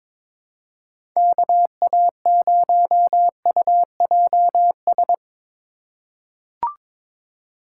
Morse code, heard as a keyed tone, about 22 wpm, 700 Hz. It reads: KA0UJS
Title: USA Call Signs - 6 Characters - 2x Character Spacing - 22wpm